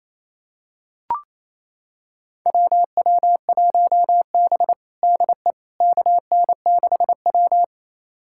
Morse code, heard as a keyed tone, about 28 wpm, 700 Hz. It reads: WW1B DE KN6W
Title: USA Callsign DE Callsign - 4 Characters - 28wpm